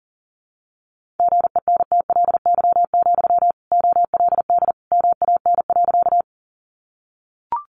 Morse code, heard as a keyed tone, about 40 wpm, 700 Hz. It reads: GENTLY, OLD MAN.
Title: Sentences from Top 1500 Words 40wpm